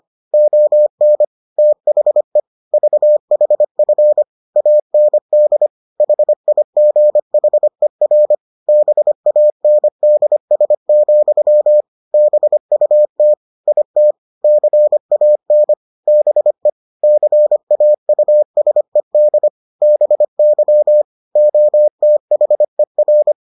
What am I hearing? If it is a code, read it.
Morse code, heard as a keyed tone, about 25 wpm, 600 Hz. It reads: ON THE VHF AND HIGHER BANDS, BUT IT CAN BE CAUSED BY OTHER